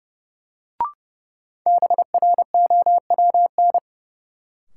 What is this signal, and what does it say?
Morse code, heard as a keyed tone, about 30 wpm, 700 Hz. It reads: BROWN
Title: Sentences from Top 1200 Words - New Word Drill Alt - 30wpm